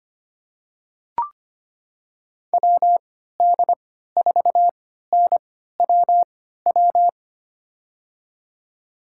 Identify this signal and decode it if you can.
Morse code, heard as a keyed tone, about 25 wpm, spaced as for 14 wpm, 700 Hz. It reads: WD4NWW
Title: USA Call Signs - 6 Characters - 3x Character Spacing - 25wpm